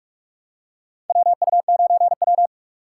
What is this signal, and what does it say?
Morse code, heard as a keyed tone, about 44 wpm, 700 Hz. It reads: WU9W
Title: USA Call Signs - 4 Characters - 8x Character Spacing - 45wpm